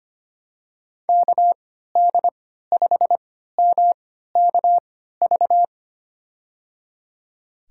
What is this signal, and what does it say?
Morse code, heard as a keyed tone, about 25 wpm, spaced as for 14 wpm, 700 Hz. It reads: KD5MKV
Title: USA Call Signs - 3x Character Spacing - 25wpm